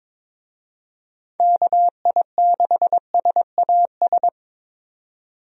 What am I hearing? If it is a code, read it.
Morse code, heard as a keyed tone, about 22 wpm, 700 Hz. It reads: KI6SAS
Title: USA Call Signs - 6 Characters - 22wpm